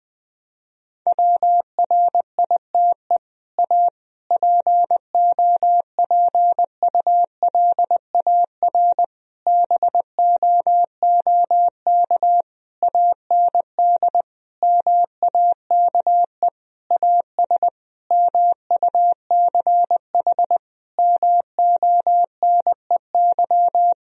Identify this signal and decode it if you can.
Morse code, heard as a keyed tone, about 20 wpm, 700 Hz. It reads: WRITE A POPULAR BOOK AND MAKE AS MUCH MONEY
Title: Sentences from Top 1000 Words - Review Part 3 - 20wpm